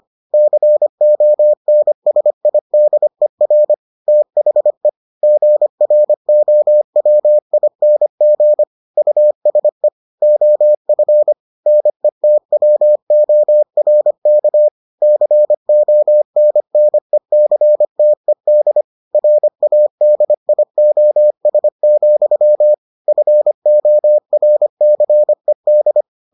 Morse code, heard as a keyed tone, about 25 wpm, 600 Hz. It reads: CONSIDER THE GROWING USE OF NETWORK CONNECTED RADIOS, FORCED